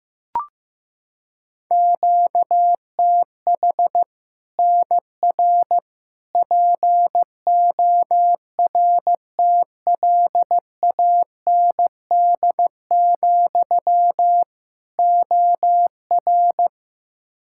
Morse code, heard as a keyed tone, about 16 wpm, 700 Hz. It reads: QTH NR PORTLAND, OR
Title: Practice QSO - Exchange 1 - Beginning - 15wpm